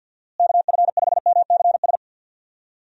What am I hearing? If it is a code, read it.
Morse code, heard as a keyed tone, about 51 wpm, 700 Hz. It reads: KF5MCS
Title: USA Call Signs - 6 Characters - Repeated 2x - 50wpm